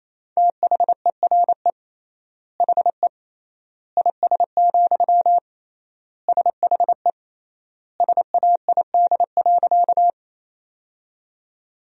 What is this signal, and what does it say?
Morse code, heard as a keyed tone, about 28 wpm, 700 Hz. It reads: THERE HE IS, SHE SAID.
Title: 3x Word-Spacing - Sentences from Top 100 Words - 28wpm